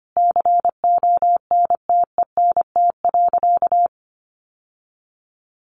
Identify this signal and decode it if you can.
Morse code, heard as a keyed tone, about 25 wpm, 700 Hz. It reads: CONTENT.
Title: Sentences from Top 1400 Words 25wpm